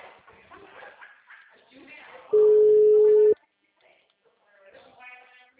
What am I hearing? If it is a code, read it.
Ringback tone (Europe, 425 Hz), with speech.